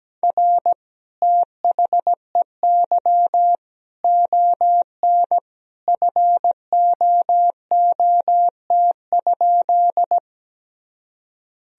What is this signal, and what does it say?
Morse code, heard as a keyed tone, about 17 wpm, 700 Hz. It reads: R THEY ON FOOT?
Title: Sentences from Top 500 Words and Abbreviations 17wpm